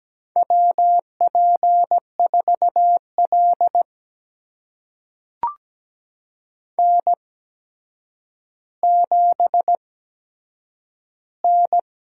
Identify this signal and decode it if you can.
Morse code, heard as a keyed tone, about 17 wpm, spaced as for 8 wpm, 700 Hz. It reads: WP4L N7N
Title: USA Call Signs - 4 Characters - 8x Character Spacing - 17wpm